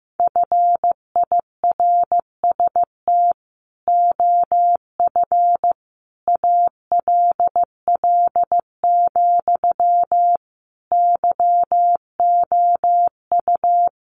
Morse code, heard as a keyed tone, about 15 wpm, 700 Hz. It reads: FIRST OF ALL, YOU
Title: Sentences from Top 1500 Words 15wpm